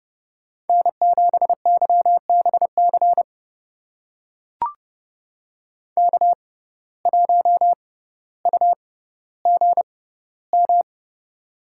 Morse code, heard as a keyed tone, about 30 wpm, spaced as for 15 wpm, 700 Hz. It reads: N7YBC K1UGM
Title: USA Call Signs - 6x Character Spacing - 30wpm